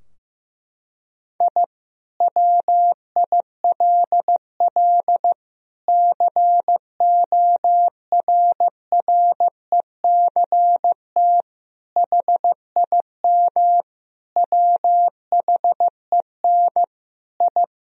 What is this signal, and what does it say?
Morse code, heard as a keyed tone, about 16 wpm, 700 Hz. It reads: I WILL CORRECT HIM WHEN I
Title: Sentences from Top 1000 Words - Review Part 2 - 15wpm